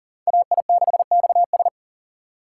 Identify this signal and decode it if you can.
Morse code, heard as a keyed tone, about 49 wpm, 700 Hz. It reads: AI6XS
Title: Speed Racing - USA 2x2 Call Signs 40wpm